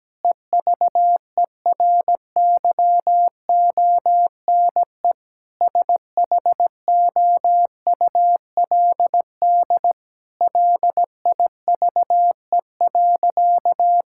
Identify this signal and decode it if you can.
Morse code, heard as a keyed tone, about 17 wpm, 700 Hz. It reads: EVERYONE SHOULD LIVE.
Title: Sentences from Top 1000 Words - Review Part 5 - 17wpm